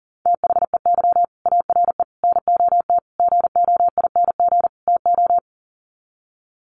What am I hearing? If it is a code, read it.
Morse code, heard as a keyed tone, about 40 wpm, 700 Hz. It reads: THEY ARE NOT GOING TO
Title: Sets of 5 Words Encore2 40wpm